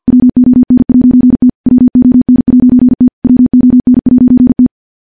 Morse code, heard as a keyed tone, about 50 wpm, 250 Hz. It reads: WON'T WON'T WON'T
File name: morse / morse-training-50WPM